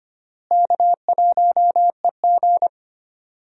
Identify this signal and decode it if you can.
Morse code, heard as a keyed tone, about 25 wpm, 700 Hz. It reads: K1EG
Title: USA Call Signs - 4 Characters - Repeated 3x - 25wpm